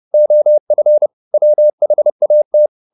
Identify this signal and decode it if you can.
Morse code, heard as a keyed tone, about 30 wpm, 600 Hz. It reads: OF WHAT
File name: the old man and the sea / the old man and the sea 1 30wpm